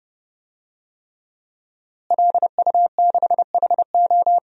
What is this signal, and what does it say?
Morse code, heard as a keyed tone, about 30 wpm, 700 Hz. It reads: LU6HO
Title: DXCC-100 Call Signs - 5 Characters - Repeated 3x - 30wpm